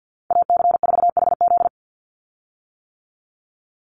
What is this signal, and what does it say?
Morse code, heard as a keyed tone, about 49 wpm, 700 Hz. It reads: AC4HZ
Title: USA Call Signs 50wpm